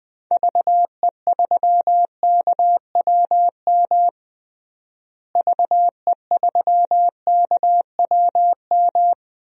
Morse code, heard as a keyed tone, about 20 wpm, 700 Hz. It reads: VE3KWM VE3KWM
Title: DXCC-100 Call Signs - 6 Characters - Repeated 3x - 20wpm